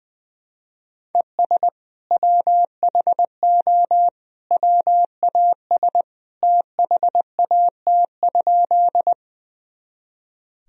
Morse code, heard as a keyed tone, about 20 wpm, 700 Hz. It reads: ES WHO WAS THAT?
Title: Sentences from Top 100 Words and Abbreviations 20wpm